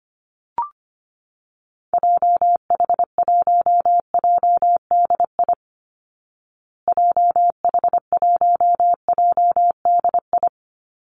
Morse code, heard as a keyed tone, about 25 wpm, 700 Hz. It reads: JH1JDI JH1JDI
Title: Call Signs Repeated 2x - 25wpm